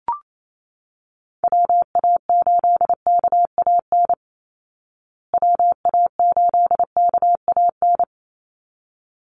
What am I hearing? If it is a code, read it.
Morse code, heard as a keyed tone, about 28 wpm, 700 Hz. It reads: WA8KAN WA8KAN
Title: Call Signs Repeated 3x - 28wpm